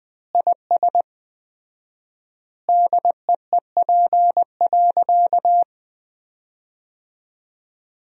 Morse code, heard as a keyed tone, about 20 wpm, 700 Hz. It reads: IS DEEP.
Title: 4x Word-Spacing - Sentences from Top 500 Words - 20wpm